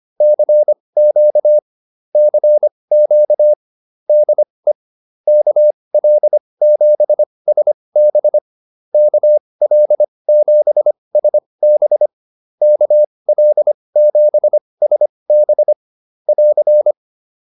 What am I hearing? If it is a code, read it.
Morse code, heard as a keyed tone, about 25 wpm, spaced as for 20 wpm, 600 Hz. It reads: CQ CQ DE KL7SB KL7SB KL7SB <AR>